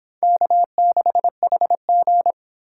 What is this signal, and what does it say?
Morse code, heard as a keyed tone, about 24 wpm, 700 Hz. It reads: K6HG